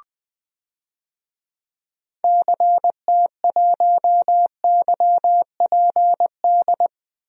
Morse code, heard as a keyed tone, about 20 wpm, 700 Hz. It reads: CT1YPD
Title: DXCC-100 Call Signs - 6 Characters - 20wpm